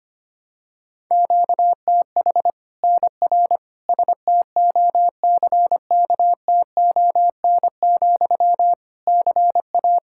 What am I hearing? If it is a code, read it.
Morse code, heard as a keyed tone, about 25 wpm, 700 Hz. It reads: QTH NR STOCKTON, CA
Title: Practice QSO - Exchange 1 - Beginning - 25wpm